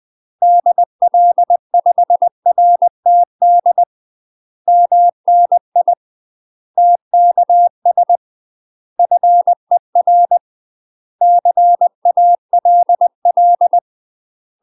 Morse code, heard as a keyed tone, about 20 wpm, 700 Hz. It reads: DL5RTD MNI TKS FER CALL
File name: UK MASTER QSOs TRAINING / 20WPM / UK MASTER QSOs 0068